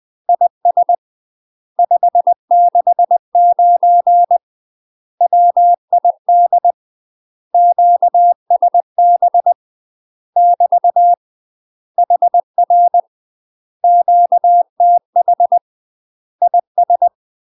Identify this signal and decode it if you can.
Morse code, heard as a keyed tone, about 20 wpm, 700 Hz. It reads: IS 569 WID QSB <BT> HR QTH IS
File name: UK MASTER QSOs TRAINING / 20WPM / UK MASTER QSOs 0015